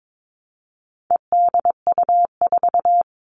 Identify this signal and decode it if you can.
Morse code, heard as a keyed tone, about 22 wpm, 700 Hz. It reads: EDU4
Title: All Cut Numbers - 4-Digit 22wpm